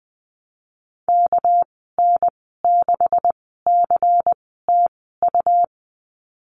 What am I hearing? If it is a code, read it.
Morse code, heard as a keyed tone, about 20 wpm, spaced as for 14 wpm, 700 Hz. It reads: KN6CTU
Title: 213 Lesson 34g - Number 8 - Call Signs 20wpm